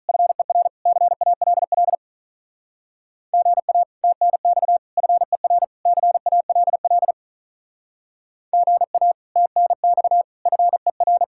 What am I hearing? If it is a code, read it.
Morse code, heard as a keyed tone, about 48 wpm, 700 Hz. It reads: FER CALL GA TNX FER CALL GA TNX FER